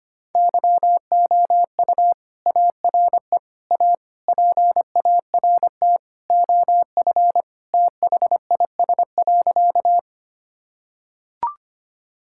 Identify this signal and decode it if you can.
Morse code, heard as a keyed tone, about 25 wpm, 700 Hz. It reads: YOU ARE A PART OF THIS.
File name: practice / 25wpm / Sentences with 3-Word Phrases - 25wpm